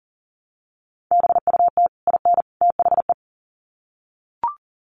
Morse code, heard as a keyed tone, about 40 wpm, 700 Hz. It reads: BUT IN THE
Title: Sets of 3 Words 40wpm